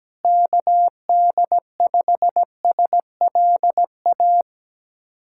Morse code, heard as a keyed tone, about 17 wpm, 700 Hz. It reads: KD5SLA
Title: USA Callsign DE Callsign - 6 Characters - 17wpm